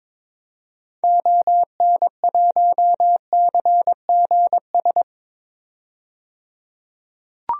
Morse code, heard as a keyed tone, about 22 wpm, 700 Hz. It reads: ON1CGS
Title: DXCC-100 Call Signs - 6 Characters - 22wpm